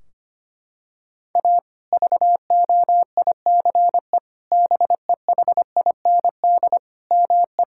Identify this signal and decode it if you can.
Morse code, heard as a keyed tone, about 25 wpm, 700 Hz. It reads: A VOICE BEHIND ME
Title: Sentences from Top 1600 Words 25wpm